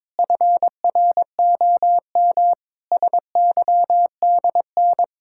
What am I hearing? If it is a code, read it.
Morse code, heard as a keyed tone, about 22 wpm, 700 Hz. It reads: FROM SYDN
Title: Pangram Challenge v2 - 22wpm